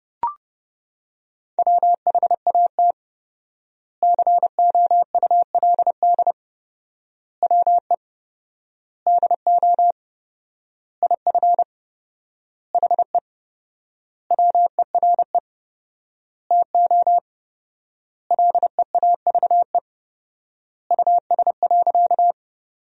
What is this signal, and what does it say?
Morse code, heard as a keyed tone, about 30 wpm, 700 Hz. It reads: WHAT COULD WE DO IF HE WERE TO LEAVE US.
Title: 4x Word-Spacing - Sentences from Top 400 Words - 30wpm